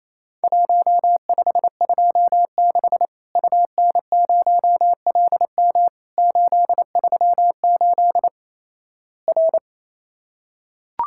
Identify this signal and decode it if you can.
Morse code, heard as a keyed tone, about 28 wpm, 700 Hz. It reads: 1526 UN0LM 838 R
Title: Worked All Europe DX Contest - QTC Exchange - 28wpm